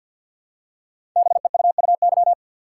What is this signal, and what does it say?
Morse code, heard as a keyed tone, about 46 wpm, 700 Hz. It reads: BEUUY
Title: Code Groups - 5 Letters - 50wpm